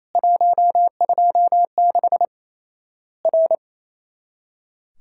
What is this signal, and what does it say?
Morse code, heard as a keyed tone, about 28 wpm, 700 Hz. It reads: 126 R
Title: Worked All Europe DX Contest - QTC Exchange - 28wpm